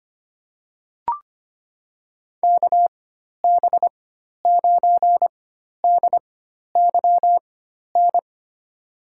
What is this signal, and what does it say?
Morse code, heard as a keyed tone, about 25 wpm, spaced as for 12 wpm, 700 Hz. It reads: KB9DYN
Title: USA Call Signs - 4x Character Spacing - 25wpm